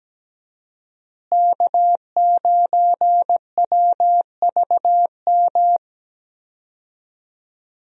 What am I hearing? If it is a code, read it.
Morse code, heard as a keyed tone, about 17 wpm, 700 Hz. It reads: K9WVM
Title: USA Call Signs - 5 Characters - Repeated 3x - 17wpm